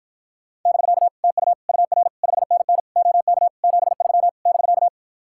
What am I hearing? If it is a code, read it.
Morse code, heard as a keyed tone, about 52 wpm, 700 Hz. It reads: <BK> TU UR 5NN OK 73 <BK>